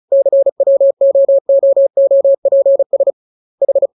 Morse code, heard as a keyed tone, about 34 wpm, 550 Hz. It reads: CWOOOPS H
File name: jokes / joke 24 35 0000